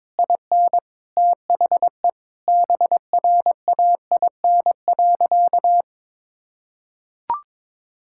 Morse code, heard as a keyed tone, about 22 wpm, 700 Hz. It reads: IN THE BRAIN.
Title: Sentences from Top 1000 Words - Review Part 6 - 22wpm